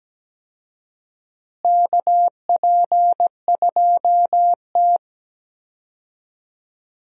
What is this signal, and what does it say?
Morse code, heard as a keyed tone, about 17 wpm, 700 Hz. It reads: KP2T
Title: USA Call Signs - 4 Characters - Repeated 3x - 17wpm